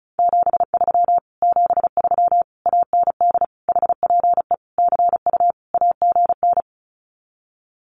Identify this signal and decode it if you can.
Morse code, heard as a keyed tone, about 36 wpm, 700 Hz. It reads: 73 73 AND HPE CU AGN
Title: Common QSO Phrases Encore 35wpm